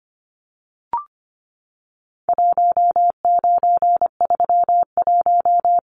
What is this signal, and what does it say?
Morse code, heard as a keyed tone, about 25 wpm, 700 Hz. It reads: 1931
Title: Numbers - 4-Digit 25wpm